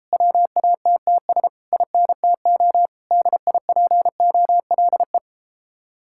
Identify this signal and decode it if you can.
Morse code, heard as a keyed tone, about 38 wpm, 700 Hz. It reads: WATTS INTO DIPOLE